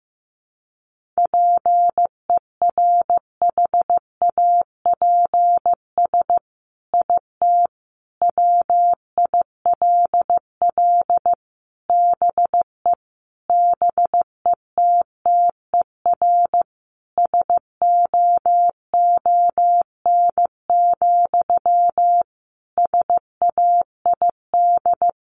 Morse code, heard as a keyed tone, about 15 wpm, 700 Hz. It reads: PERHAPS IT WILL BE BETTER SOON, SAID
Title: Sentences from Top 1500 Words 15wpm